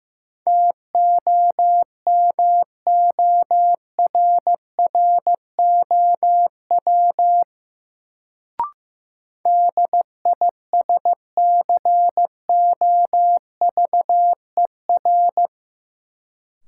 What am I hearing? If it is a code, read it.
Morse code, heard as a keyed tone, about 15 wpm, 700 Hz. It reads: TOMORROW DISCOVER
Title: Sentences from Top 1100 Words - New Word Drill - 15wpm